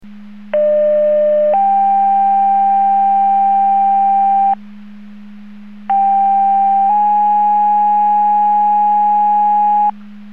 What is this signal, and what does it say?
Two-tone page, heard as a one-second tone, then a three-second tone, about 600 then 790 Hz, about 790 then 830 Hz. It is heard from 0.5 to 9.9 s.